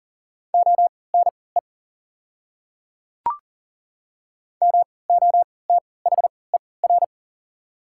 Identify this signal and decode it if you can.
Morse code, heard as a keyed tone, about 40 wpm, spaced as for 23 wpm, 690 Hz. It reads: ONE MOTHER
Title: 3x Character Spacing - Top 200 Words - 40wpm